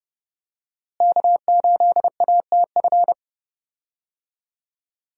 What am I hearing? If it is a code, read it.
Morse code, heard as a keyed tone, about 31 wpm, 700 Hz. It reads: K8ATF
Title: USA Call Signs - 5 Characters - Repeated 2x - 30wpm